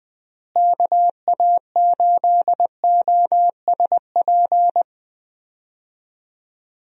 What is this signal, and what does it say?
Morse code, heard as a keyed tone, about 21 wpm, 700 Hz. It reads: KA8OSP
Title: USA Call Signs - 6 Characters - Repeated 2x - 20wpm